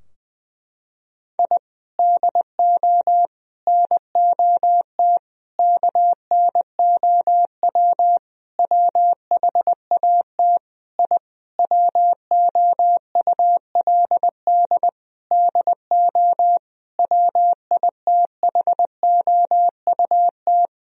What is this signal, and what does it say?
Morse code, heard as a keyed tone, about 20 wpm, 700 Hz. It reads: I DO NOT KNOW WHAT I WOULD DO WITHOUT
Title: Sentences from Top 1000 Words - Review Part 3 - 20wpm